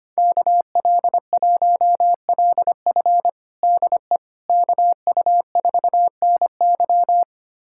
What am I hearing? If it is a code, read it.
Morse code, heard as a keyed tone, about 25 wpm, 700 Hz. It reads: KL1LF DE KU4NY
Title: USA Callsign DE Callsign - 5 Characters - 25wpm